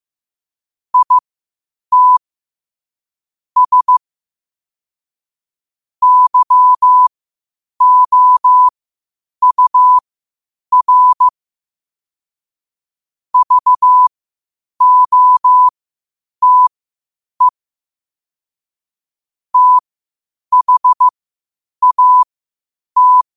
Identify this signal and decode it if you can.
Morse code, heard as a keyed tone, about 15 wpm, spaced as for 8 wpm, 1000 Hz. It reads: IT S YOUR VOTE THAT